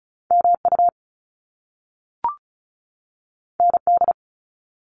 Morse code, heard as a keyed tone, about 36 wpm, 700 Hz. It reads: MU ND
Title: Top Two Letter Combinations in Words 35wpm